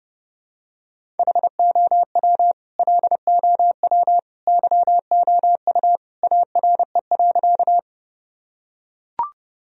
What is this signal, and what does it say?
Morse code, heard as a keyed tone, about 30 wpm, 700 Hz. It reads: HOW LOW YOU ARE.
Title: Sentences from Top 1000 Words - Review Part 1 - 30wpm